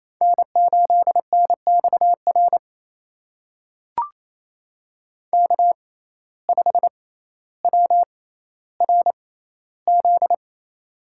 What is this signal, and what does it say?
Morse code, heard as a keyed tone, about 28 wpm, spaced as for 14 wpm, 700 Hz. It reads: N8NXR K5WRZ
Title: USA Call Signs - 5 Characters - 6x Character Spacing - 28wpm